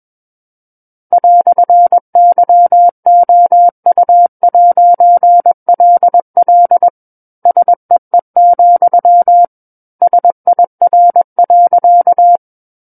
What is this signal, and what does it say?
Morse code, heard as a keyed tone, about 21 wpm, 700 Hz. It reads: "YOU'LL SEE, SIR.
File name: Book One / 12-Chapter 12 - 21 WPM